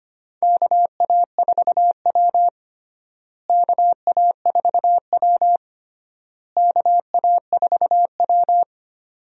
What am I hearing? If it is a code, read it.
Morse code, heard as a keyed tone, about 25 wpm, 700 Hz. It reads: KA4W KA4W KA4W
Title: USA Call Signs - 4 Characters - Repeated 3x - 25wpm